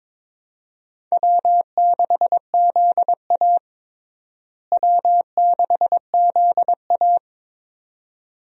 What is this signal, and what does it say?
Morse code, heard as a keyed tone, about 21 wpm, 700 Hz. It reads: W6ZA W6ZA
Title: USA Call Signs - 4 Characters - Repeated 2x - 22wpm